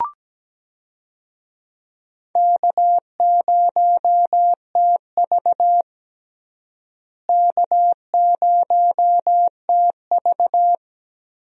Morse code, heard as a keyed tone, about 17 wpm, 700 Hz. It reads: K0TV K0TV
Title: USA Call Signs - 4 Characters - Repeated 3x - 17wpm